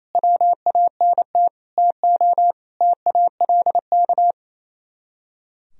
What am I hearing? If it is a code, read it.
Morse code, heard as a keyed tone, about 35 wpm, 700 Hz. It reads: WANT TO TALK